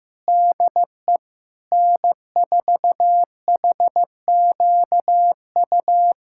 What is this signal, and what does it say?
Morse code, heard as a keyed tone, about 15 wpm, 700 Hz. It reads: DE N4HQU
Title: SOTA QSO Practice 15wpm